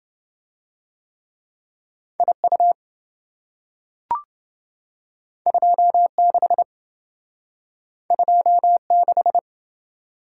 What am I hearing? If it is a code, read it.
Morse code, heard as a keyed tone, about 26 wpm, 700 Hz. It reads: IU 26 26